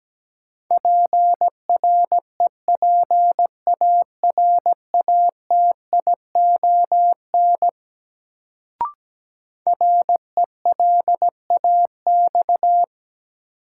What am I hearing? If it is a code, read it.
Morse code, heard as a keyed tone, about 17 wpm, 700 Hz. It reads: PREPARATION RELAX
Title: Sentences from Top 1400 Words - New Word Drill - 17wpm